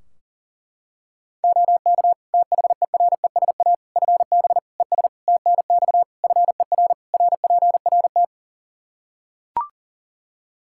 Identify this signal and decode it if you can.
Morse code, heard as a keyed tone, about 40 wpm, 700 Hz. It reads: OK THERESA FB ES TNX FER RPRT
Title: Practice QSO - Exchange 2 - Info - 40wpm